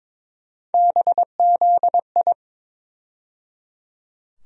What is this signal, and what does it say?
Morse code, heard as a keyed tone, about 22 wpm, 700 Hz. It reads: BZI
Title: USA Call Sign Postfixes - 3 Letters 22wpm